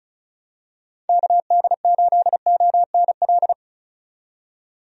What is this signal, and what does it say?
Morse code, heard as a keyed tone, about 36 wpm, 700 Hz. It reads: KD8ONL
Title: USA Call Signs - 6 Characters - 35wpm